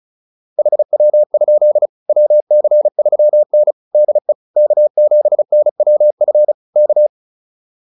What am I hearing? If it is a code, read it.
Morse code, heard as a keyed tone, about 34 wpm, 600 Hz. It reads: HW? WC3N DE K7NWF K
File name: QSO209 35